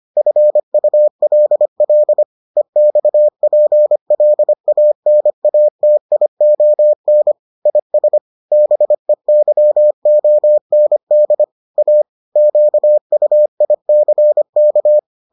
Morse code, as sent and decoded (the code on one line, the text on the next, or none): ..-. ..- .-.. .-.. / . -..- .--. .-.. .- -. .- - .. --- -. / .. ... / -... -.-- --- -. -.. / .- / --.- ..- .. -.-. -.-
FULL EXPLANATION IS BEYOND A QUICK